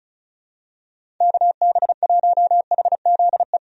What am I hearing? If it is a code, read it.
Morse code, heard as a keyed tone, about 34 wpm, 700 Hz. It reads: KB1HZE
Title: USA Call Signs - 6 Characters - 35wpm